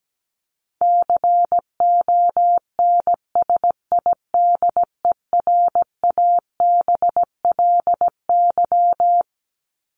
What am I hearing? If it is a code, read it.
Morse code, heard as a keyed tone, about 17 wpm, 700 Hz. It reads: CONSIDERABLY
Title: LY Suffix - Top 100 Words with Base Words 17wpm